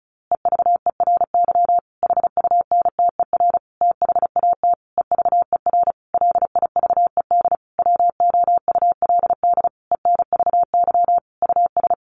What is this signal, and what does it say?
Morse code, heard as a keyed tone, about 34 wpm, 700 Hz. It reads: EVERY HUNTER THAT EVER LIVED WOULD ENVY US
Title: A Sound Of Thunder 35wpm